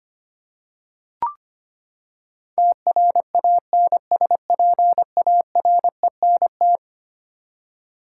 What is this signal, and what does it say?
Morse code, heard as a keyed tone, about 25 wpm, 700 Hz. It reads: TRANSPARENT
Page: morse-code-ninja-practice-25wpm